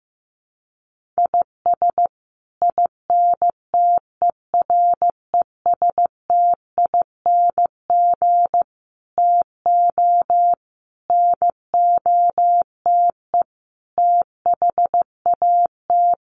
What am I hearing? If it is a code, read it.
Morse code, heard as a keyed tone, about 15 wpm, 700 Hz. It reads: IS INTERESTING TO NOTE THAT
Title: Sets of 5 Words Encore2 15wpm